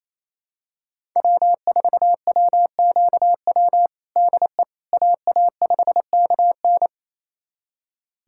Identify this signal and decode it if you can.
Morse code, heard as a keyed tone, about 27 wpm, 700 Hz. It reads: W4WQW DE AA5KN
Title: USA Callsign DE Callsign - 5 Characters - 28wpm